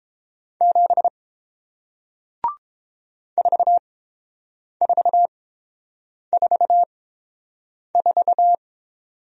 Morse code, heard as a keyed tone, about 30 wpm, spaced as for 7 wpm, 700 Hz. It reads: 7 4444